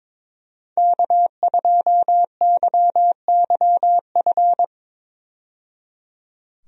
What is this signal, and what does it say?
Morse code, heard as a keyed tone, about 22 wpm, 700 Hz. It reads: K2YYF
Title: USA Call Signs - 5 Characters - 22wpm